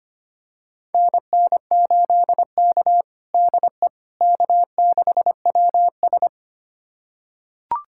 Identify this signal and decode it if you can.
Morse code, heard as a keyed tone, about 25 wpm, 700 Hz. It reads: NN8K DE K6WS
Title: USA Callsign DE Callsign - 4 Characters - 25wpm